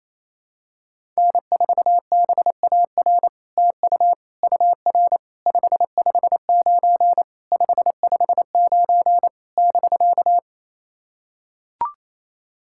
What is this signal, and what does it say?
Morse code, heard as a keyed tone, about 28 wpm, 700 Hz. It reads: N4BAR TU UR 559 559 <BK>
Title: SOTA QSO Practice 28wpm